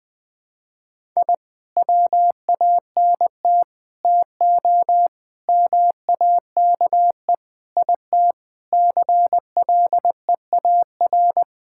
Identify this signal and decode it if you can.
Morse code, heard as a keyed tone, about 20 wpm, 700 Hz. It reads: I WANT TO MAKE IT CLEAR
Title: Sentences with 7-Word Phrases - 20wpm